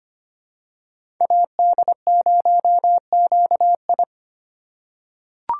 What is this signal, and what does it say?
Morse code, heard as a keyed tone, about 25 wpm, 700 Hz. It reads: AD0QI
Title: USA Call Signs - 5 Characters - 4x Character Spacing - 25wpm